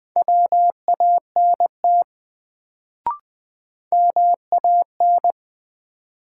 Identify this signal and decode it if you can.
Morse code, heard as a keyed tone, about 20 wpm, 700 Hz. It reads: WANT MAN
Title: Sets of 2 Words - New Word Drill - 20wpm